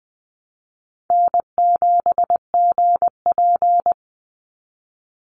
Morse code, heard as a keyed tone, about 20 wpm, 700 Hz. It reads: N7GP
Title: USA Call Signs 20wpm